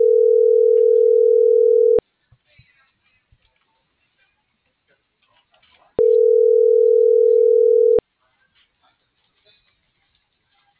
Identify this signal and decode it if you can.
Ringback tone (North America), 440 + 480 Hz.